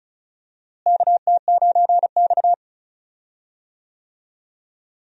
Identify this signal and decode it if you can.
Morse code, heard as a keyed tone, about 34 wpm, 700 Hz. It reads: KT9X
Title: USA Call Signs - 4 Characters - Repeated 2x - 35wpm